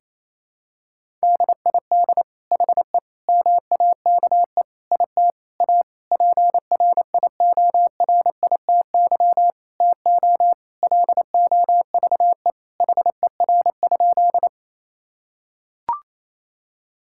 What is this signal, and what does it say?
Morse code, heard as a keyed tone, about 28 wpm, 700 Hz. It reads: DID HE MAKE IT A PRIORITY TO LOVE HER?
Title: Sentences from Top 1600 Words 28wpm